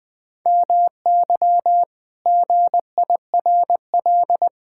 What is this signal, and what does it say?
Morse code, heard as a keyed tone, about 19 wpm, 700 Hz. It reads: MY GIRL
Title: Pangram Challenge v2 - 20wpm